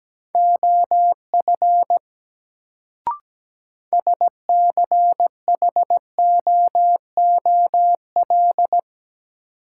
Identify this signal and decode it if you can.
Morse code, heard as a keyed tone, about 17 wpm, 700 Hz. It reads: OF SCHOOL
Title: Sets of 2 Words - New Word Drill - 17wpm